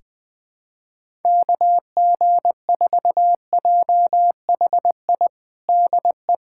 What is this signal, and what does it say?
Morse code, heard as a keyed tone, about 20 wpm, 700 Hz. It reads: KG4JHI DE
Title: USA Callsign DE Callsign 20wpm